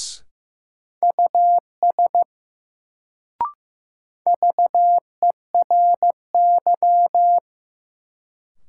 Morse code, heard as a keyed tone, about 16 wpm, 700 Hz. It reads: US VERY
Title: Sets of 2 Words - New Word Drill - 15wpm